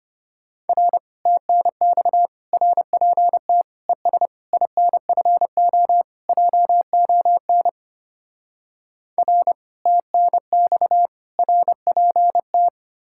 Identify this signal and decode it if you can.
Morse code, heard as a keyed tone, about 32 wpm, 700 Hz. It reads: R TNX RPT ES INFO JON R TNX RPT